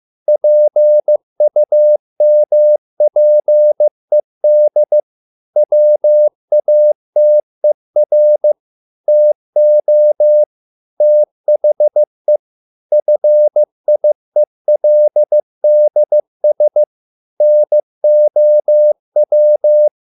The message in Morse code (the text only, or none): PUMPED WATER TO THE FIELDS NOW